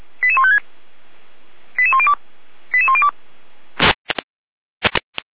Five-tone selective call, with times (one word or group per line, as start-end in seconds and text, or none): 0.2-0.6 88156
1.8-2.1 88102
2.7-3.1 88102